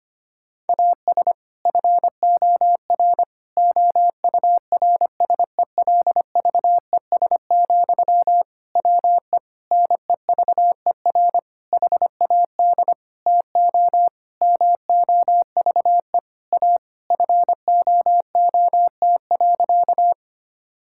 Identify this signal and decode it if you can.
Morse code, heard as a keyed tone, about 25 wpm, 700 Hz. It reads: AS FOR OURSELVES, WE NEVER HAD TO MOVE A FOOT.
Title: Sentences from Top 1600 Words 25wpm